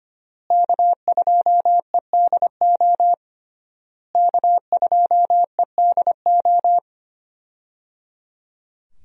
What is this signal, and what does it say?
Morse code, heard as a keyed tone, about 25 wpm, 700 Hz. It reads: K2EDO K2EDO
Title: USA Call Signs - 5 Characters - Repeated 3x - 25wpm